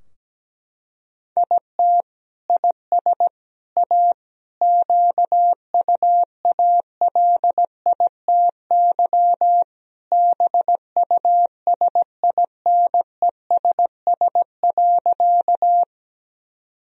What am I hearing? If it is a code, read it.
Morse code, heard as a keyed tone, about 17 wpm, 700 Hz. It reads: IT IS A QUALITY BUSINESS.
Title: Sentences from Top 1000 Words - Review Part 4 - 17wpm